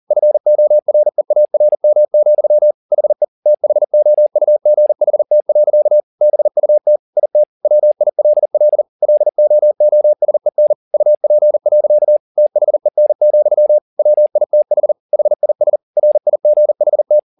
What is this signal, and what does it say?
Morse code, heard as a keyed tone, about 40 wpm, 600 Hz. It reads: FOREARM, HE THOUGHT. BUT IT WILL LOOSEN UP. THEN, WITH HIS RIGHT